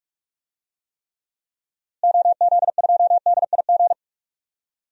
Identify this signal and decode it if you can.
Morse code, heard as a keyed tone, about 46 wpm, 700 Hz. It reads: OZ2DIG
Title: DXCC-100 Call Signs - 6 Characters - Repeated 2x - 45wpm